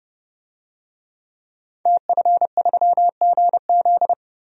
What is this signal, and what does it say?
Morse code, heard as a keyed tone, about 30 wpm, 700 Hz. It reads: TF3GZ
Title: DXCC-100 Call Signs - 5 Characters - Repeated 3x - 30wpm